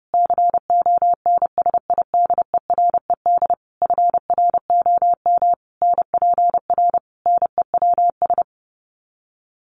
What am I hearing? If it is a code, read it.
Morse code, heard as a keyed tone, about 28 wpm, 700 Hz. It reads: CONSIDERED FROM NPR NEWS
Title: Sets of 5 Words Encore 30wpm